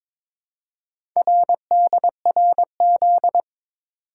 Morse code, heard as a keyed tone, about 23 wpm, 700 Hz. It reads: RDRZ
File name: practice / 22wpm / Four Letter-Number Combinations - 22wpm